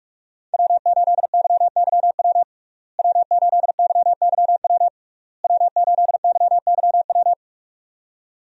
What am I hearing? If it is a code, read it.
Morse code, heard as a keyed tone, about 43 wpm, 700 Hz. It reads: W8YYW W8YYW W8YYW
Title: USA Call Signs - 5 Characters - Repeated 3x - 45wpm